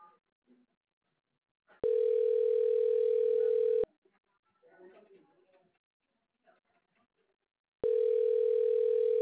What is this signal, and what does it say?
Ringback tone (North America), 440 + 480 Hz.